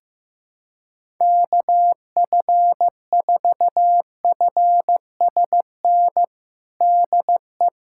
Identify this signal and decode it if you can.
Morse code, heard as a keyed tone, about 15 wpm, 700 Hz. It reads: KF4FSN DE
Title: USA Callsign DE Callsign - 6 Characters - 15wpm